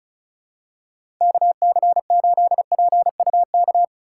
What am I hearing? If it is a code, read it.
Morse code, heard as a keyed tone, about 34 wpm, 700 Hz. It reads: KC8PUK
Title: USA Call Signs - 6 Characters - 8x Character Spacing - 35wpm